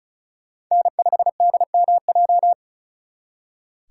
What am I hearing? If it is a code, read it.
Morse code, heard as a keyed tone, about 45 wpm, 700 Hz. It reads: N5DMJ